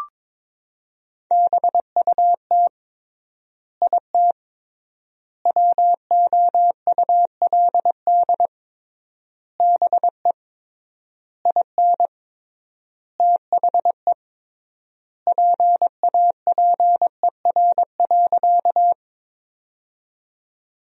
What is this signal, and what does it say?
Morse code, heard as a keyed tone, about 22 wpm, 700 Hz. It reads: BUT IT WOULD BE IN THE PAPER.
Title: 3x Word-Spacing - Sentences from Top 300 Words - 22wpm